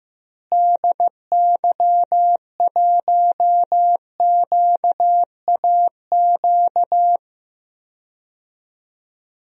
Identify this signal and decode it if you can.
Morse code, heard as a keyed tone, about 15 wpm, 700 Hz. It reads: DY1QAQ
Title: DXCC-100 Call Signs - 6 Characters - Repeated 2x - 15wpm